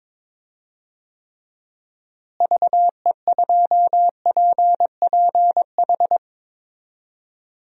Morse code, heard as a keyed tone, about 22 wpm, 700 Hz. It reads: VE2PPH
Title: DXCC-100 Call Signs - 6 Characters - Repeated 3x - 22wpm